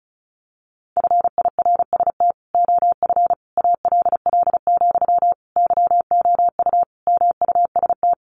Morse code, heard as a keyed tone, about 34 wpm, 700 Hz. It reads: FIRST OF ALL, YOU MUST
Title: Sentences from Top 1500 Words 35wpm